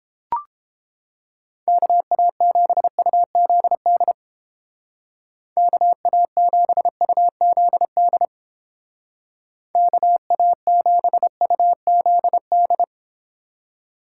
Morse code, heard as a keyed tone, about 29 wpm, 700 Hz. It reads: KA7UZD KA7UZD KA7UZD